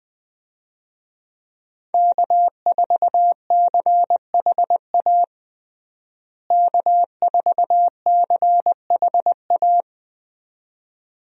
Morse code, heard as a keyed tone, about 20 wpm, 700 Hz. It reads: K4CHA K4CHA
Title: USA Call Signs - 5 Characters - Repeated 2x - 20wpm